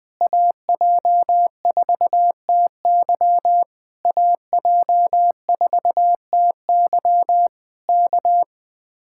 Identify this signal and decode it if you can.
Morse code, heard as a keyed tone, about 20 wpm, 700 Hz. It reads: AJ4TY AJ4TY K
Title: Speed-Racing Common QSO Phrases Encore - 20wpm